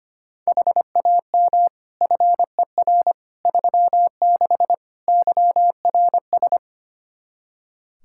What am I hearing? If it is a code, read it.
Morse code, heard as a keyed tone, about 25 wpm, 700 Hz. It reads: HAM FER 36 YRS
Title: Practice QSO - Exchange 3 - More Info - 25wpm